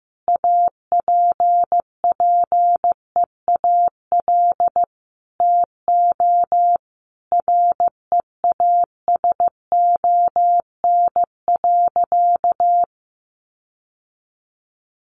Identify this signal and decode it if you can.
Morse code, heard as a keyed tone, about 15 wpm, 700 Hz. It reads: APPEAL TO REASON.
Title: Sentences from Top 1300 Words 15wpm